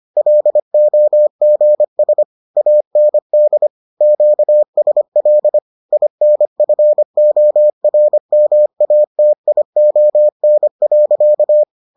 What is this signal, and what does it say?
Morse code, heard as a keyed tone, about 25 wpm, 600 Hz. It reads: LOGS AND QSL INFORMATION.